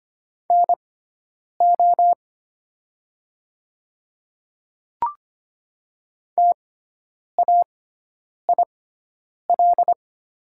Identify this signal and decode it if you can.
Morse code, heard as a keyed tone, about 25 wpm, spaced as for 9 wpm, 700 Hz. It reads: NO TAIL
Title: 6x Character Spacing - Top 500 Words - 25wpm